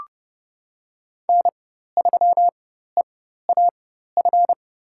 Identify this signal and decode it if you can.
Morse code, heard as a keyed tone, about 32 wpm, spaced as for 14 wpm, 700 Hz. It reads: N3EAF
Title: USA Call Signs - 5 Characters - 4x Character Spacing - 30wpm